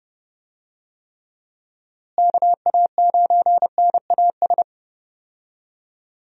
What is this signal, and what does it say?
Morse code, heard as a keyed tone, about 30 wpm, 700 Hz. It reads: KA9NAS
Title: USA Call Signs - 6 Characters - Repeated 2x - 30wpm